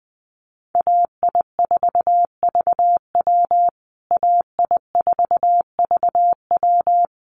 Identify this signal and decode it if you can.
Morse code, heard as a keyed tone, about 20 wpm, 700 Hz. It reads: AI4VW AI4VW
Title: USA Call Signs Repeated 3x 20wpm